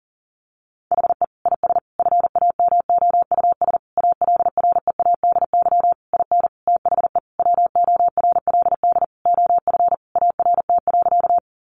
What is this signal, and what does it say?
Morse code, heard as a keyed tone, about 40 wpm, 700 Hz. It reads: HE IS FAMOUS ALREADY IN THE WORLD OF ART.
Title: Sentences from Top 1500 Words 40wpm